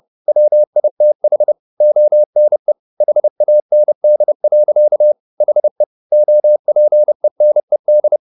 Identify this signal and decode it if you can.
Morse code, heard as a keyed tone, about 30 wpm, 600 Hz. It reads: WITH ONE HAND. HE OPENED
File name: the old man and the sea / the old man and the sea 1 30wpm